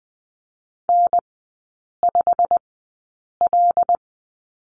Morse code, heard as a keyed tone, about 20 wpm, spaced as for 8 wpm, 690 Hz. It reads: N5L
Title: 087 Lesson 15e - Number 5 - Three Characters 20wpm